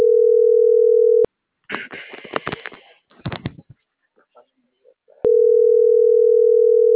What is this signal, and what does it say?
Ringback tone (North America), 440 + 480 Hz.